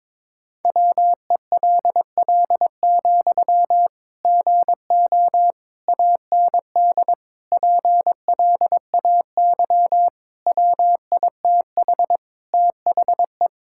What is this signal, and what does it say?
Morse code, heard as a keyed tone, about 22 wpm, 700 Hz. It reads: WELL, GO AND PLAY WITH THE
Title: Sentences from Top 1600 Words 22wpm